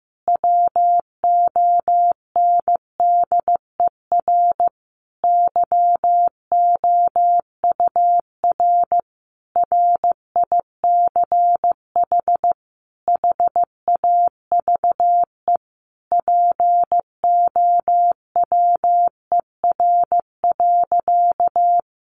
Morse code, heard as a keyed tone, about 15 wpm, 700 Hz. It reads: WONDER YOUR RICH HAVE POWER.
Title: Sentences from Top 1100 Words 15wpm